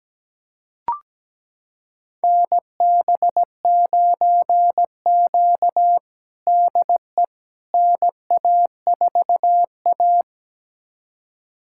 Morse code, heard as a keyed tone, about 17 wpm, 700 Hz. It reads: NB9Q DE NA4A
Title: USA Callsign DE Callsign - 4 Characters - 17wpm